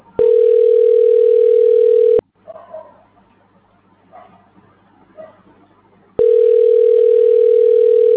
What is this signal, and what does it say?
Ringback tone (North America), 440 + 480 Hz.